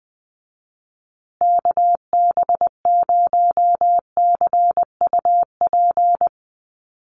Morse code, heard as a keyed tone, about 20 wpm, 700 Hz. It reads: KB0CUP
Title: USA Call Signs Repeated 3x 20wpm